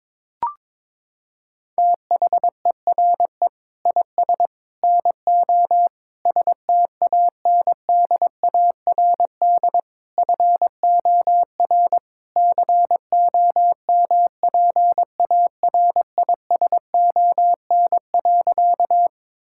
Morse code, heard as a keyed tone, about 21 wpm, 700 Hz. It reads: THERE IS NO STANDARD FOR COMPARISON.
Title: Sentences from Top 1600 Words 22wpm